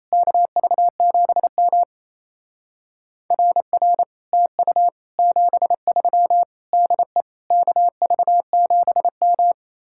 Morse code, heard as a keyed tone, about 35 wpm, 700 Hz. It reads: KV7M RR TU 73 DE KV7M
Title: Speed-Racing Common QSO Phrases - 28wpm